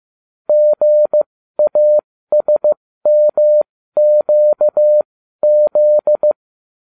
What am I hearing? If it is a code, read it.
Morse code, heard as a keyed tone, about 15 wpm, spaced as for 13 wpm, 600 Hz. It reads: GASMQZ